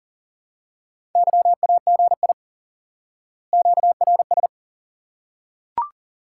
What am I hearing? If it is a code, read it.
Morse code, heard as a keyed tone, about 40 wpm, 700 Hz. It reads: YAGI QRS
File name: course / instant-qso-element-course-3 / IQE Course 3 - Lesson 89 - YAGI - Two Elements - 40wpm